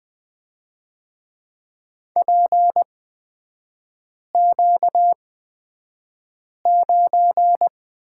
Morse code, heard as a keyed tone, about 20 wpm, spaced as for 5 wpm, 700 Hz. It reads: PQ9
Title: 250 Lesson 40e - BK - Three Characters 20wpm